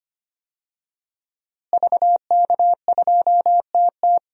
Morse code, heard as a keyed tone, about 25 wpm, 700 Hz. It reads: VK2TT
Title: DXCC-100 Call Signs - 5 Characters - Repeated 3x - 25wpm